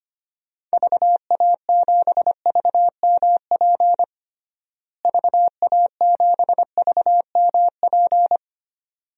Morse code, heard as a keyed tone, about 25 wpm, 700 Hz. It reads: VA7VMP VA7VMP
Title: DXCC-100 Call Signs - 6 Characters - Repeated 2x - 25wpm